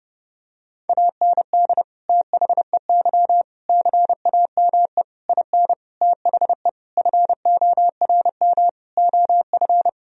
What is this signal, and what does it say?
Morse code, heard as a keyed tone, about 30 wpm, 700 Hz. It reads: AND THEY CAME IN THE FORM OF
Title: Sentences from Top 1000 Words - Review Part 1 - 30wpm